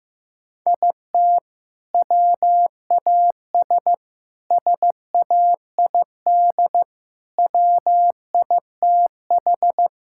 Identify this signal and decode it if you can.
Morse code, heard as a keyed tone, about 15 wpm, 700 Hz. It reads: IT WAS SAID WITH
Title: Sentences from Top 1600 Words 15wpm